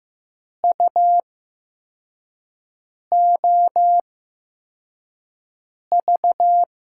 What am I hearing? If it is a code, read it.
Morse code, heard as a keyed tone, about 15 wpm, spaced as for 4 wpm, 700 Hz. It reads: UOV
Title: USA Call Signs - 8x Character Spacing - 15wpm